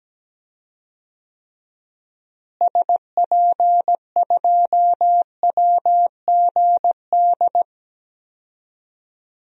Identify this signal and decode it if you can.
Morse code, heard as a keyed tone, about 17 wpm, 700 Hz. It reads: SP2WGD
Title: DXCC-100 Call Signs - 6 Characters - Repeated 2x - 17wpm